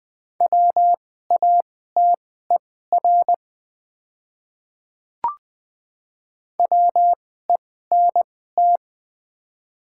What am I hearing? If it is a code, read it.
Morse code, heard as a keyed tone, about 20 wpm, spaced as for 15 wpm, 690 Hz. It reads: WATER WENT
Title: 2x Character Spacing - Top 200 Words - 20wpm